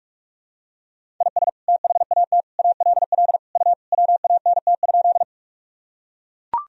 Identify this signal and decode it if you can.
Morse code, heard as a keyed tone, about 42 wpm, 700 Hz. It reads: IS THAT ALL U WANT?
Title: Sentences from Top 200 Words and Abbreviations 45wpm